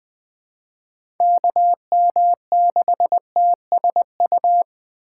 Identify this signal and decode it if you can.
Morse code, heard as a keyed tone, about 20 wpm, 700 Hz. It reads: KM6TSU
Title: USA Call Signs - 6 Characters - 8x Character Spacing - 20wpm